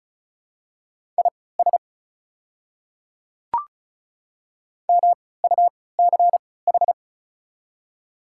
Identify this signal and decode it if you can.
Morse code, heard as a keyed tone, about 36 wpm, spaced as for 20 wpm, 700 Hz. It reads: IS MUCH
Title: 3x Character Spacing - Top 300 Words - 35wpm